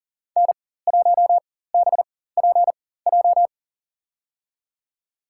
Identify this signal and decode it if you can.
Morse code, heard as a keyed tone, about 40 wpm, spaced as for 19 wpm, 700 Hz. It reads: N1BPJ
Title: USA Call Signs - 4x Character Spacing - 40wpm